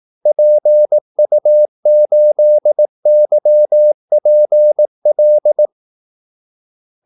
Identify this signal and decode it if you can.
Morse code, heard as a keyed tone, about 18 wpm, 600 Hz. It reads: PU8YPL
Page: Kaldesignaler 18 wpm